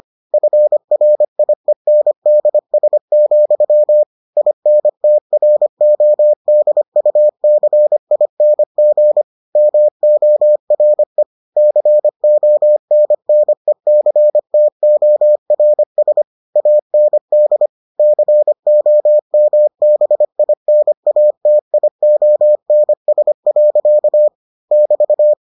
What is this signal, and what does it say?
Morse code, heard as a keyed tone, about 25 wpm, 600 Hz. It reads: FRIENDS, INTRODUCING MORE CONNECTORS AND COMBINATIONS. <BT>